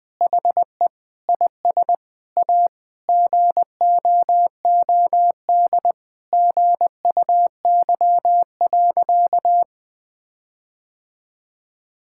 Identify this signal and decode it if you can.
Morse code, heard as a keyed tone, about 20 wpm, 700 Hz. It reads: HE IS A GOOD GUY.
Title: Sentences with 8-Word Phrases - Spoken Beginning - 20wpm